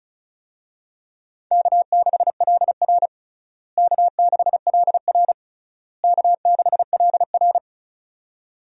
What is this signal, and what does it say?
Morse code, heard as a keyed tone, about 36 wpm, 700 Hz. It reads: K6LR K6LR K6LR
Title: USA Call Signs - 4 Characters - Repeated 3x - 35wpm